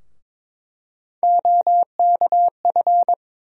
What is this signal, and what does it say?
Morse code, heard as a keyed tone, about 22 wpm, 700 Hz. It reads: OKF
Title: USA Call Sign Postfixes - 3 Letters 22wpm